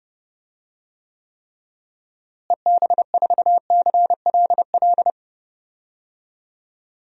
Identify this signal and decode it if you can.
Morse code, heard as a keyed tone, about 30 wpm, 700 Hz. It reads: EB4CLL
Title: DXCC-100 Call Signs - 6 Characters - Repeated 3x - 30wpm